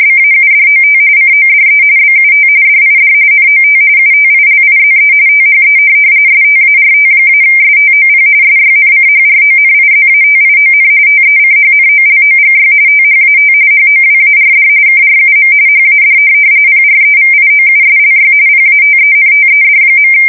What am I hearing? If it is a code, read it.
RTTY, about 45.45 baud. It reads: BAUDOT CODE BUT DID NOT UTILIZE THE START-STOP SYNCHRONIZATION TECHNOLOGY THAT HOWARD KRUM HAD PREVIOUSLY PATENTED.
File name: fsk170 / teletype-corp